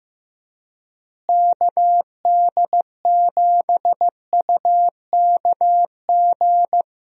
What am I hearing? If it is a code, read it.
Morse code, heard as a keyed tone, about 15 wpm, 700 Hz. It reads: KD7UKG
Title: USA Call Signs - 6 Characters - Repeated 2x - 15wpm